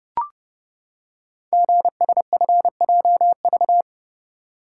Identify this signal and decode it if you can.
Morse code, heard as a keyed tone, about 29 wpm, 700 Hz. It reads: GSFJV
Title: Code Groups - 5 Letters - 30wpm